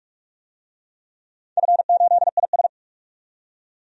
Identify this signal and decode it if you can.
Morse code, heard as a keyed tone, about 44 wpm, 700 Hz. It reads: F8IS
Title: DXCC-100 Call Signs - 4 Characters - 45wpm